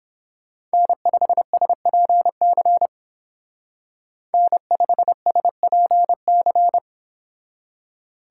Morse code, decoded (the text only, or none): N5SPC N5SPC